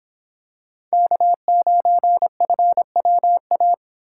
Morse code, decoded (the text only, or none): K9FWA